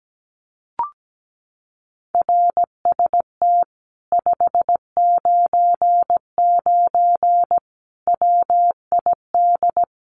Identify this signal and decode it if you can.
Morse code, heard as a keyed tone, about 17 wpm, 700 Hz. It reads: RST 599 WID
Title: Common QSO Phrases Encore 17wpm